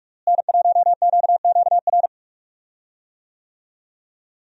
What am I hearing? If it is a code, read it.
Morse code, heard as a keyed tone, about 46 wpm, 700 Hz. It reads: N1QQR
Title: USA Call Signs - 5 Characters - Repeated 2x - 45wpm